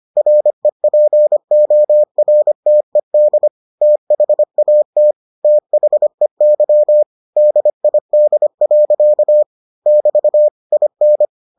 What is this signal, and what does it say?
Morse code, heard as a keyed tone, about 25 wpm, 600 Hz. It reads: REPORTED THAT THEY DID. <BT> IN